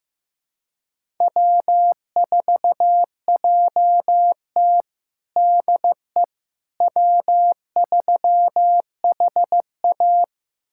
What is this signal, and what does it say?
Morse code, heard as a keyed tone, about 15 wpm, 700 Hz. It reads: W4JT DE W3HA
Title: USA Callsign DE Callsign - 4 Characters - 15wpm